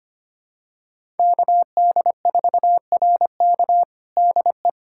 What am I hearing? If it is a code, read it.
Morse code, heard as a keyed tone, about 25 wpm, 700 Hz. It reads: KD4RK DE
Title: USA Callsign DE Callsign 25wpm